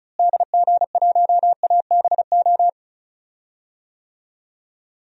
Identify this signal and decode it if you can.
Morse code, heard as a keyed tone, about 36 wpm, 700 Hz. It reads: DG1ABO
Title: DXCC-100 Call Signs - 6 Characters - Repeated 3x - 35wpm